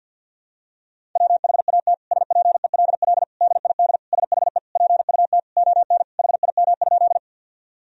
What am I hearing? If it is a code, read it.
Morse code, heard as a keyed tone, about 49 wpm, 700 Hz. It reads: WHAT SPELL DID SHE PUT ON HIM?
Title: Sentences from Top 1000 Words - Review Part 1 - 50wpm